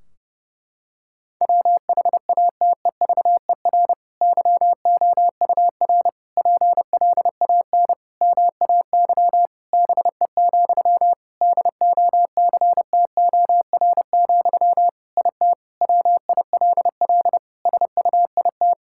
Morse code, heard as a keyed tone, about 32 wpm, 700 Hz. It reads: WHATEVER YOUR PLAN MAY BE, DOCTOR, IT WILL SUIT
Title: Sentences from Top 1600 Words 30wpm